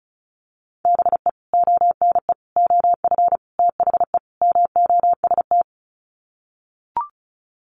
Morse code, heard as a keyed tone, about 34 wpm, 700 Hz. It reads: BE ONE OF THE MOST
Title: Sets of 5 Words Encore 35wpm